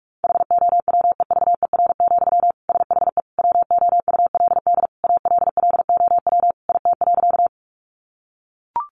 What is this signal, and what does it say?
Morse code, heard as a keyed tone, about 44 wpm, 700 Hz. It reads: HOWEVER, SHE WOULD ALLOW IT.
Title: Sentences from Top 700 Words 45wpm